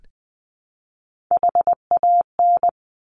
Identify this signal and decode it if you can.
Morse code, heard as a keyed tone, about 20 wpm, 700 Hz. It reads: HAN
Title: Top Three Letter Combinations in Words 20wpm